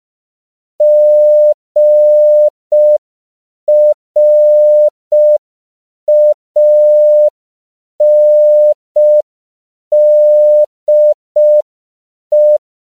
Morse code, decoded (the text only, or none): GRANDE